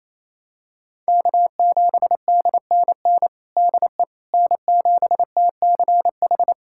Morse code, heard as a keyed tone, about 28 wpm, 700 Hz. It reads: K7DNN DE N7TCH
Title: USA Callsign DE Callsign - 5 Characters - 28wpm